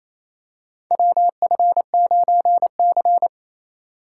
Morse code, heard as a keyed tone, about 28 wpm, 700 Hz. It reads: WF9C
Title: USA Call Signs - 4 Characters - 3x Character Spacing - 28wpm